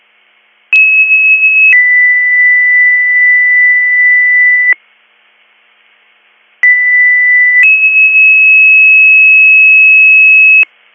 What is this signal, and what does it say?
Two-tone page, heard as a one-second tone, then a three-second tone, about 2600 then 2000 Hz, about 2000 then 2600 Hz. It is heard from 0.7 to 10.6 s.